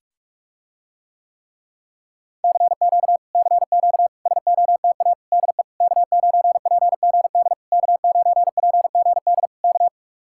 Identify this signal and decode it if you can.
Morse code, heard as a keyed tone, about 45 wpm, 700 Hz. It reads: CQ CQ SOTA DE K9PGD K9PGD K